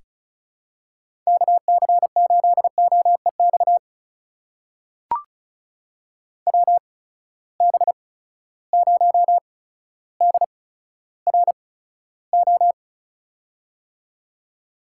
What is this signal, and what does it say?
Morse code, heard as a keyed tone, about 36 wpm, spaced as for 15 wpm, 700 Hz. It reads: KC8OEX WB0DRO
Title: USA Call Signs - 6 Characters - 8x Character Spacing - 35wpm